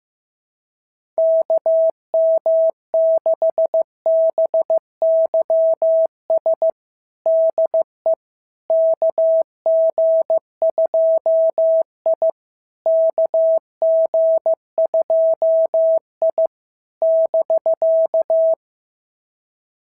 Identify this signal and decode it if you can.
Morse code, heard as a keyed tone, about 14 wpm, 650 Hz. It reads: KM6BYS DE KG2I KG2I <BK>
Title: SOTA QSO Practice 15wpm